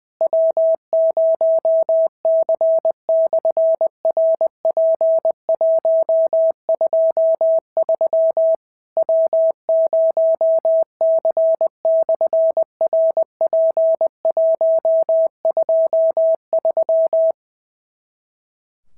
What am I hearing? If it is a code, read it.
Morse code, heard as a keyed tone, about 20 wpm, 650 Hz. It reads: W0C/RP123 W0C/RP123
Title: SOTA QSO Practice 20wpm